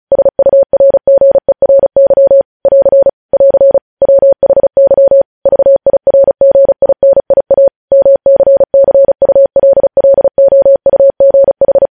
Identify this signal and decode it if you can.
Morse code, heard as a keyed tone, about 36 wpm, 550 Hz. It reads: SURGERY <AR> <AR> WHY VIRGINIA MCCULLOUGH